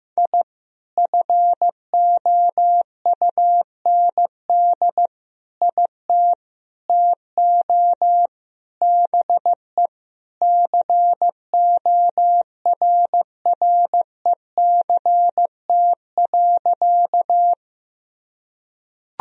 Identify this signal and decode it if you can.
Morse code, heard as a keyed tone, about 15 wpm, 700 Hz. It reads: I FOUND IT TO BE CORRECT.
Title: Sentences from Top 1000 Words - Review Part 2 - 15wpm